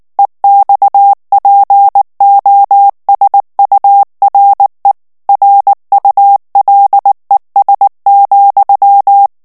Morse code, heard as a keyed tone, about 19 wpm, 800 Hz. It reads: EXPOSURE RULES,
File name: ANGLAIS texte clair 3 vitesse 18 mots minute